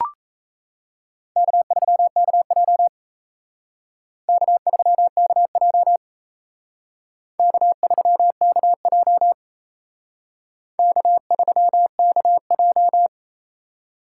Morse code, heard as a keyed tone, about 35 wpm, 700 Hz. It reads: K3KJ K3KJ K3KJ K3KJ
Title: Speed Racing - USA 1x2 Call Signs 28wpm